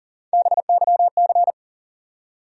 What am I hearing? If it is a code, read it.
Morse code, heard as a keyed tone, about 40 wpm, 700 Hz. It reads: BYC
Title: USA Call Sign Postfixes - 3 Letters 40wpm